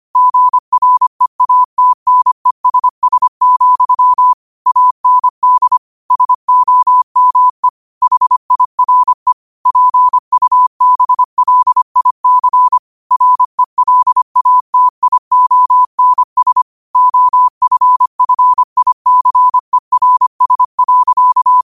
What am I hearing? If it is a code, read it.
Morse code, heard as a keyed tone, about 25 wpm, 1000 Hz. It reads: GREATNESS, AND SOME HIRE PUBLIC RELATIONS OFFICERS.
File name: qotdfiles / 25wpm